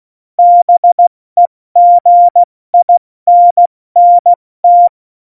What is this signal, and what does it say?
Morse code, heard as a keyed tone, about 16 wpm, 700 Hz. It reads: BEGINNT
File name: Morsekurs / Witze / Witz 19. Februar 2019, 70 F80